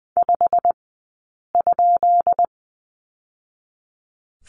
Morse code, heard as a keyed tone, about 20 wpm, spaced as for 8 wpm, 700 Hz. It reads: 5?
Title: 151 Lesson 25d - Letter p - Two Characters 20wpm